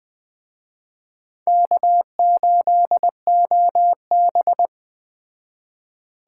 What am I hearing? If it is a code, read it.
Morse code, heard as a keyed tone, about 21 wpm, 700 Hz. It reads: K8OB